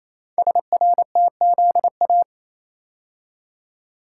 Morse code, heard as a keyed tone, about 27 wpm, 700 Hz. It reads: SRTZA
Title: Code Groups - 5 Letters - 28wpm